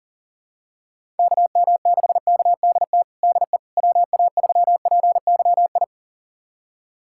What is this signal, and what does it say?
Morse code, heard as a keyed tone, about 40 wpm, 700 Hz. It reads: KM6KDT DE WA3PYI
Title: USA Callsign DE Callsign - 6 Characters - 40wpm